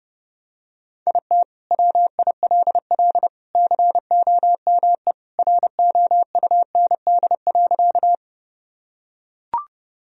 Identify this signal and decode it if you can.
Morse code, heard as a keyed tone, about 30 wpm, 700 Hz. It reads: IT WILL COME ROUND.
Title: Sentences from Top 1000 Words - Review Part 1 - 30wpm